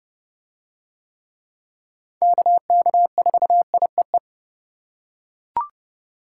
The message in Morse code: KK4IEE

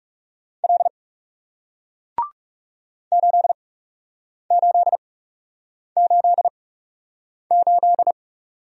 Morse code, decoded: L 8888

